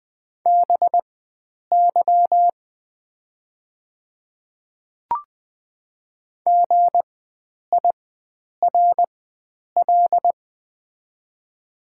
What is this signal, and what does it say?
Morse code, heard as a keyed tone, about 20 wpm, spaced as for 9 wpm, 700 Hz. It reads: BY GIRL
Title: 4x Character Spacing - Top 500 Words - 20wpm